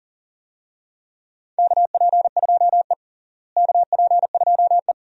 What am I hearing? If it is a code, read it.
Morse code, heard as a keyed tone, about 40 wpm, 700 Hz. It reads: KP2E KP2E
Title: DXCC-100 Call Signs - 4 Characters - Repeated 3x - 40wpm